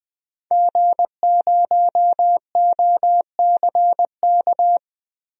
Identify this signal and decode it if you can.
Morse code, heard as a keyed tone, about 20 wpm, 700 Hz. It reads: G0OCK
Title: DXCC-100 Call Signs - 5 Characters - Repeated 2x - 20wpm